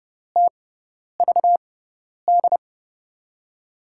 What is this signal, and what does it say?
Morse code, heard as a keyed tone, about 30 wpm, spaced as for 10 wpm, 700 Hz. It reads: TVD
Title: USA Call Signs - 6 Characters - 6x Character Spacing - 30wpm